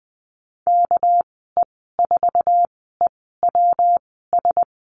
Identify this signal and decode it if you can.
Morse code, heard as a keyed tone, about 20 wpm, spaced as for 15 wpm, 700 Hz. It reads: KE4EWS
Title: 200 Lesson 32g - Letter k - Call Signs 20wpm